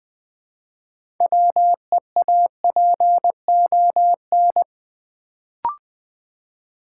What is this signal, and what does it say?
Morse code, heard as a keyed tone, about 20 wpm, 700 Hz. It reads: WEAPON
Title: Build A Word - 2 Letters At A Time - 6-Letter Words - 20wpm